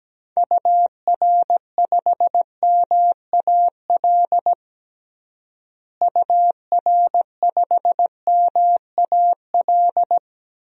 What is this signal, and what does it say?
Morse code, heard as a keyed tone, about 17 wpm, 700 Hz. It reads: UR5MAL UR5MAL
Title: DXCC-100 Call Signs - 6 Characters - Repeated 3x - 17wpm